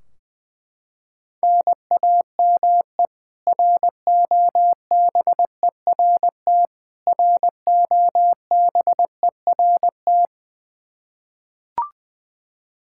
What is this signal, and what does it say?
Morse code, heard as a keyed tone, about 20 wpm, 700 Hz. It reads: NAME ROBERT ROBERT
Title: Practice QSO - Exchange 1 - Beginning - 20wpm